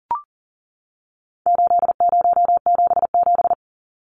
Morse code, heard as a keyed tone, about 40 wpm, 700 Hz. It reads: 8077
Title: Numbers - 4-Digit 40wpm